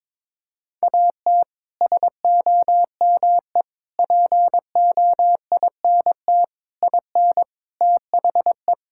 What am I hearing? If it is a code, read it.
Morse code, heard as a keyed tone, about 22 wpm, 700 Hz. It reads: AT SOME POINT IN THE
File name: practice / 22wpm / Sets of 6 Words - 22wpm